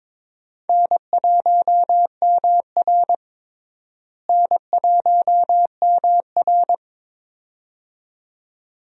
Morse code, heard as a keyed tone, about 22 wpm, 700 Hz. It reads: N1MR N1MR
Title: USA Call Signs - 4 Characters - Repeated 2x - 22wpm